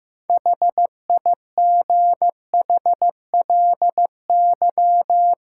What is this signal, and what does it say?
Morse code, heard as a keyed tone, about 15 wpm, 700 Hz. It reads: HIGHLY
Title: Sentences from Top 1100 Words - New Word Drill - 15wpm